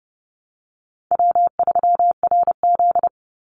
Morse code, heard as a keyed tone, about 30 wpm, 700 Hz. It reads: W3RZ
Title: Easy Call Signs Repeated 2x - 30wpm